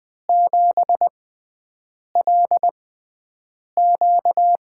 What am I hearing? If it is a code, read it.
Morse code, heard as a keyed tone, about 20 wpm, spaced as for 7 wpm, 700 Hz. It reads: 7LQ
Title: USA Call Signs - 4 Characters - 6x Character Spacing - 20wpm